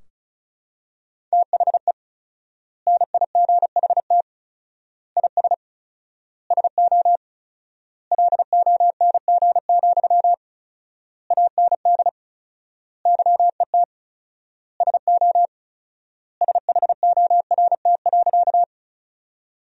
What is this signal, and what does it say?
Morse code, heard as a keyed tone, about 34 wpm, 700 Hz. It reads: THE NIGHT IS SO LONG, AND YET SO SHORT.
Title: 4x Word-Spacing - Sentences from Top 500 Words - 35wpm